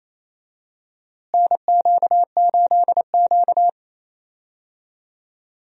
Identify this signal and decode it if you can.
Morse code, heard as a keyed tone, about 28 wpm, 700 Hz. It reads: NQ8Q
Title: USA Call Signs - 4 Characters - Repeated 2x - 28wpm